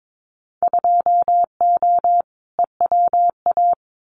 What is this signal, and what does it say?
Morse code, heard as a keyed tone, about 22 wpm, 700 Hz. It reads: 2O EWA
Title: Winter Field Day Exchanges Encore 22wpm